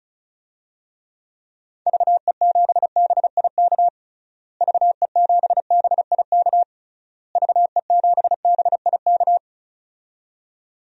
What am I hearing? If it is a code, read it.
Morse code, heard as a keyed tone, about 36 wpm, 700 Hz. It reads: VE7BIK VE7BIK VE7BIK
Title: DXCC-100 Call Signs - 6 Characters - Repeated 3x - 35wpm